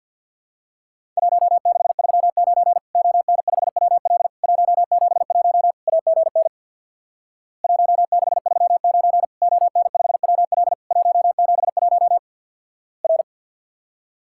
Morse code, heard as a keyed tone, about 50 wpm, 700 Hz. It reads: 1639 ON5WL 171 AGN 1639 ON5WL 171 R
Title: Worked All Europe DX Contest - QTC Exchange with Repeats - 50wpm